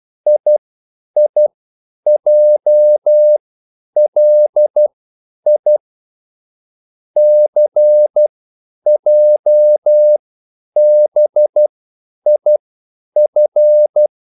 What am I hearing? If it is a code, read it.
Morse code, heard as a keyed tone, about 12 wpm, spaced as for 9 wpm, 600 Hz. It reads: IIJLI CJBIF